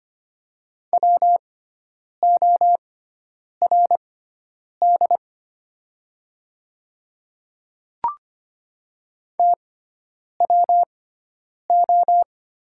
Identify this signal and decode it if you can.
Morse code, heard as a keyed tone, about 25 wpm, spaced as for 9 wpm, 700 Hz. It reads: WORD TWO
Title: 6x Character Spacing - Top 100 Words - 25wpm